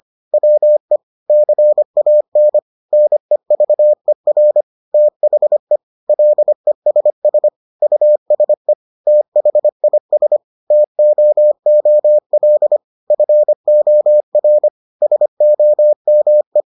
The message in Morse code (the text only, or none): WE CAN NEVER THE LESS USE THIS TOOL FOR SOME